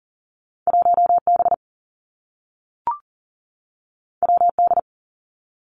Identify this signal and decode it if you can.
Morse code, heard as a keyed tone, about 38 wpm, 700 Hz. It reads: 1 B W D
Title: Two Challenging Letter-Number Combinations 40wpm